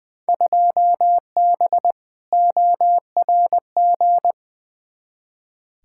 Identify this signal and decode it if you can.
Morse code, heard as a keyed tone, about 20 wpm, 700 Hz. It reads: 2B ORG
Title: Field Day Exchanges Encore - 20wpm